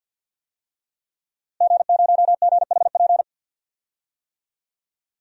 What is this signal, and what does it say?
Morse code, heard as a keyed tone, about 52 wpm, 700 Hz. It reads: G0GHP
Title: DXCC-100 Call Signs - 5 Characters - Repeated 2x - 50wpm